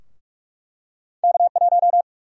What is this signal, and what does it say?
Morse code, heard as a keyed tone, about 41 wpm, 700 Hz. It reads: K1
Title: USA Call Sign Prefixes - 45wpm